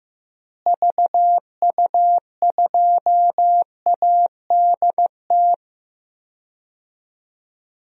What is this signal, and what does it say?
Morse code, heard as a keyed tone, about 15 wpm, 700 Hz. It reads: VU2ADT
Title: DXCC-100 Call Signs - 6 Characters - Repeated 2x - 15wpm